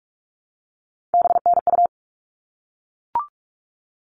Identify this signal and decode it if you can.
Morse code, heard as a keyed tone, about 44 wpm, 700 Hz. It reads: BNU
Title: All Cut Numbers - 3-Digit 45wpm